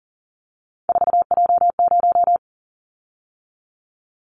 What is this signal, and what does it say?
Morse code, heard as a keyed tone, about 40 wpm, 700 Hz. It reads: 4J0
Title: Callsign Prefixes for Top 100 DXCC - 40wpm